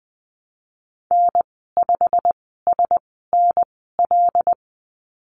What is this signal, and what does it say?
Morse code, heard as a keyed tone, about 19 wpm, spaced as for 14 wpm, 700 Hz. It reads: N5SNL
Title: 122 Lesson 20g - Number 3 - Call Signs 20wpm